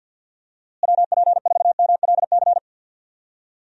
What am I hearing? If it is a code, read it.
Morse code, heard as a keyed tone, about 51 wpm, 700 Hz. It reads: WP4MLC
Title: USA Call Signs - 6 Characters - Repeated 2x - 50wpm